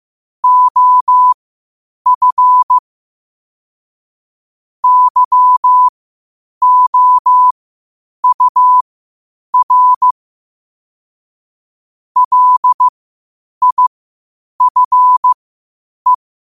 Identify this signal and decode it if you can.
Morse code, heard as a keyed tone, about 15 wpm, spaced as for 8 wpm, 1000 Hz. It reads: OF YOUR LIFE
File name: qotdfiles / 8wpm